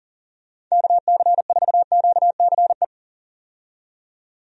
Morse code, heard as a keyed tone, about 40 wpm, 700 Hz. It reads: KC4QCE
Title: USA Call Signs - 6 Characters - 40wpm